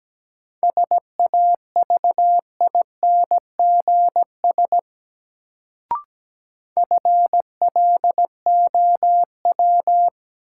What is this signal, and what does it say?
Morse code, heard as a keyed tone, about 17 wpm, 700 Hz. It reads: SAVINGS FLOW
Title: Sentences from Top 1200 Words - New Word Drill - 17wpm